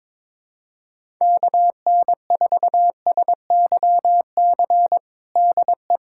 Morse code, heard as a keyed tone, about 22 wpm, 700 Hz. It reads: KN4SYC DE
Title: USA Callsign DE Callsign - 6 Characters - 22wpm